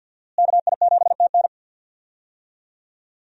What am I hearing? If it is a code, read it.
Morse code, heard as a keyed tone, about 47 wpm, 700 Hz. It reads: KI7TN
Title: USA Call Signs - 5 Characters - Repeated 2x - 50wpm